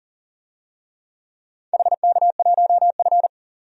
Morse code, heard as a keyed tone, about 39 wpm, 700 Hz. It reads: HK1F
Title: DXCC-100 Call Signs - 4 Characters - 40wpm